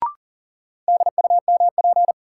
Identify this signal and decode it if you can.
Morse code, heard as a keyed tone, about 40 wpm, 700 Hz. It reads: DUMP
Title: Sentences from Top 1300 Words - New Word Drill Alt - 40wpm